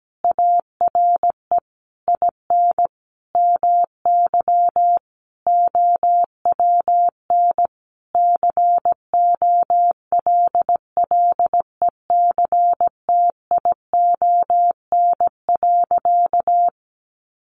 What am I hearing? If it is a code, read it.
Morse code, heard as a keyed tone, about 17 wpm, 700 Hz. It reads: ARE IN MY OWN COLLECTION.
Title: Sentences from Top 1200 Words 17wpm